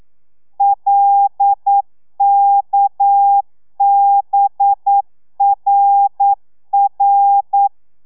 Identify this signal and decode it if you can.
Morse code, heard as a keyed tone, about 9 wpm, 800 Hz. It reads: LKBRR
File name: telegrafia / i2tfj / Lez6 40